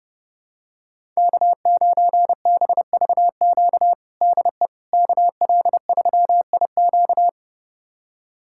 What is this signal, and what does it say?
Morse code, heard as a keyed tone, about 32 wpm, 700 Hz. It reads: K9BVQ DE KL3IQ
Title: USA Callsign DE Callsign - 5 Characters - 30wpm